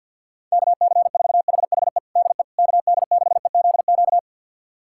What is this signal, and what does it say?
Morse code, heard as a keyed tone, about 51 wpm, 690 Hz. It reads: KC4HHE DE KD6EZQ
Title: USA Callsign DE Callsign - 6 Characters - 50wpm